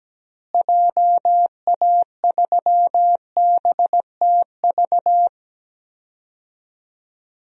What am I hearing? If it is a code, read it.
Morse code, heard as a keyed tone, about 17 wpm, 700 Hz. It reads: JA3BTV
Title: DXCC-100 Call Signs - 6 Characters - Repeated 3x - 17wpm